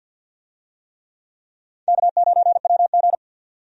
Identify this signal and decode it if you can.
Morse code, heard as a keyed tone, about 45 wpm, 700 Hz. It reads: K9WG
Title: USA Call Signs - 4 Characters - 50wpm